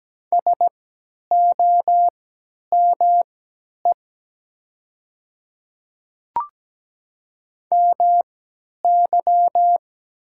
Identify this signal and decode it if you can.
Morse code, heard as a keyed tone, about 17 wpm, spaced as for 10 wpm, 700 Hz. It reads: SOME MY
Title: 3x Character Spacing - Top 200 Words - 17wpm